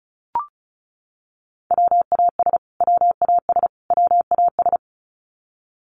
Morse code, heard as a keyed tone, about 34 wpm, 700 Hz. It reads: WAS WAS WAS
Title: Top 500 Words Repeated 35wpm